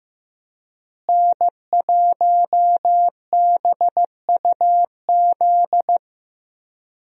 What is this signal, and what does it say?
Morse code, heard as a keyed tone, about 15 wpm, 700 Hz. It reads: N1BUZ
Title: USA Call Signs - 5 Characters - Repeated 2x - 15wpm